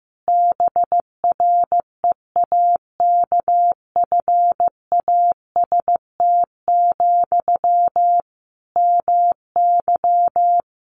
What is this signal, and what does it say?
Morse code, heard as a keyed tone, about 15 wpm, 700 Hz. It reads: BREAKFAST, MY
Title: Sentences from Top 1500 Words 15wpm